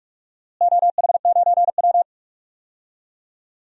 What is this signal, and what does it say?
Morse code, heard as a keyed tone, about 45 wpm, 700 Hz. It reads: OH9W
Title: DXCC-100 Call Signs - 4 Characters - Repeated 2x - 45wpm